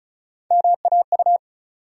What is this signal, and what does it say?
Morse code, heard as a keyed tone, about 36 wpm, 700 Hz. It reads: MAU
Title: USA Call Sign Postfixes - 3 Letters 35wpm